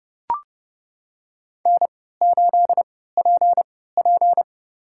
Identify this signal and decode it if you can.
Morse code, heard as a keyed tone, about 30 wpm, spaced as for 17 wpm, 700 Hz. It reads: N8PP
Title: USA Call Signs - 4 Characters - 3x Character Spacing - 30wpm